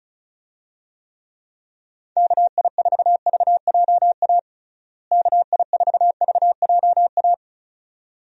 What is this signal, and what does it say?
Morse code, heard as a keyed tone, about 36 wpm, 700 Hz. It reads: KI4VJA KI4VJA
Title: USA Call Signs - 6 Characters - Repeated 2x - 35wpm